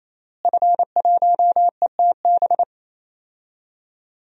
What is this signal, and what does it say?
Morse code, heard as a keyed tone, about 27 wpm, 700 Hz. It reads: F1ETB
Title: DXCC-100 Call Signs - 5 Characters - Repeated 3x - 28wpm